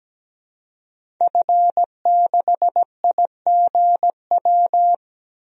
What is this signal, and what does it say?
Morse code, heard as a keyed tone, about 17 wpm, 700 Hz. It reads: F6IGW
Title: DXCC-100 Call Signs - 5 Characters - 17wpm